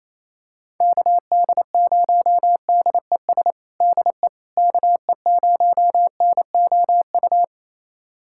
Morse code, heard as a keyed tone, about 28 wpm, 700 Hz. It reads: KD0DES DE KE0NOU
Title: USA Callsign DE Callsign - 6 Characters - 28wpm